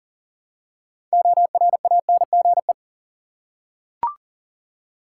Morse code, heard as a keyed tone, about 40 wpm, 700 Hz. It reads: ORANGE
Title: Six Letter Words - 40wpm